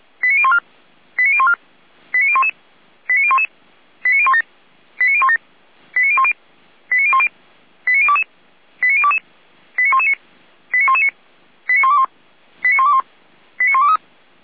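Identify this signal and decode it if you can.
Five-tone selective call (ZVEI), heard as quick runs of five tones, tones of about 70 ms. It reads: 89014 89014 89011 89011 89017 89017 89010 89010 89022 89022 89119 89119 89121 89121 89123
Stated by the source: ZVEI five-tone call 89060 received by SDR.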